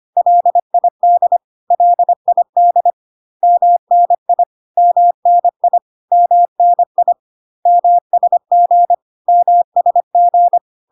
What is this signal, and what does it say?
Morse code, heard as a keyed tone, about 25 wpm, 700 Hz. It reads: LID LID MNI MNI MNI MSG MSG
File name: cw / CW common abbreviations alphabetical repeated 3 times 25 25wpm ik1hge